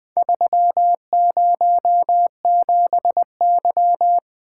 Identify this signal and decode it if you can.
Morse code, heard as a keyed tone, about 20 wpm, 700 Hz. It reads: 307Y
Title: Four Letter-Number Combinations - 20wpm